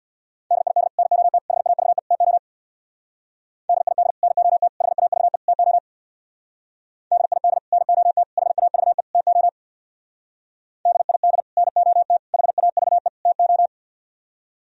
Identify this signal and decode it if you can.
Morse code, heard as a keyed tone, about 62 wpm, 690 Hz. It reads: DID NOT HAVE TO DID NOT HAVE TO DID NOT HAVE TO DID NOT HAVE TO
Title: Speed-Racing - Sets of 4 Words 50wpm